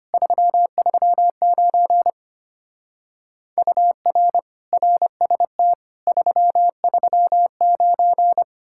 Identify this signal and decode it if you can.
Morse code, heard as a keyed tone, about 30 wpm, 700 Hz. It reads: 339 UR RST 339
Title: Speed-Racing Common QSO Phrases - 25wpm